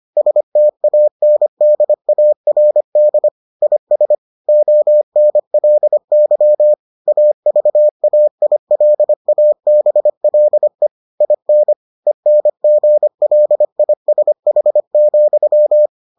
Morse code, heard as a keyed tone, about 25 wpm, 600 Hz. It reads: STANDARD IS ONLY AVAILABLE IN ENGLISH,